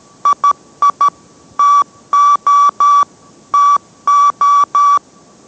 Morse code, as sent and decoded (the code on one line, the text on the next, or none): .. .. / - --- / - ---
II TO TO